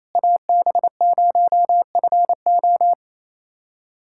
Morse code, heard as a keyed tone, about 28 wpm, 700 Hz. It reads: AB0FO
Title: USA Callsign DE Callsign - 5 Characters - 28wpm